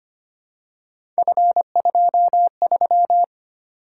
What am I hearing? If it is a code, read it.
Morse code, heard as a keyed tone, about 25 wpm, 700 Hz. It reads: F23
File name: Three Letter-Number Combinations - 25wpm